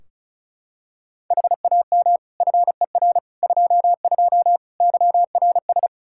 Morse code, heard as a keyed tone, about 34 wpm, 700 Hz. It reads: HAM FER 22 YRS
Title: K1USN SST QSO Practice 35wpm